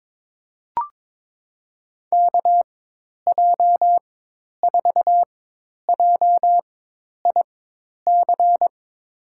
Morse code, heard as a keyed tone, about 22 wpm, spaced as for 10 wpm, 700 Hz. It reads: KJ4JIC
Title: USA Call Signs - 4x Character Spacing - 22wpm